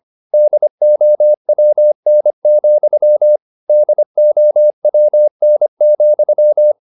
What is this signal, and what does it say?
Morse code, heard as a keyed tone, about 25 wpm, 600 Hz. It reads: DOWN, DOWN,
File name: the old man and the sea part 1 25wpm